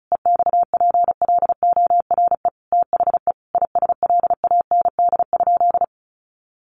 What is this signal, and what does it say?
Morse code, heard as a keyed tone, about 34 wpm, 700 Hz. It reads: EXPLORE THE ISLAND?
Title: Sentences from Top 1100 Words 35wpm